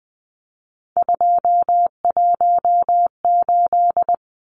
Morse code, heard as a keyed tone, about 20 wpm, 700 Hz. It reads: 218
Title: Numbers - 3-Digit 20wpm